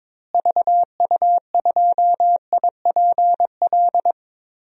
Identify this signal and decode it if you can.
Morse code, heard as a keyed tone, about 22 wpm, 700 Hz. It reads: VU2IPL
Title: DXCC-100 Call Signs - 6 Characters - Repeated 3x - 22wpm